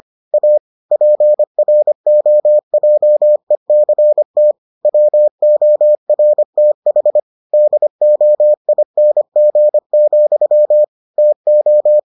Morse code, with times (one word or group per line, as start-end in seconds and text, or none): 0.3-0.6 A
0.9-4.5 PROJECT
4.8-7.2 WORTH
7.5-10.8 DOING,
11.2-12.0 TO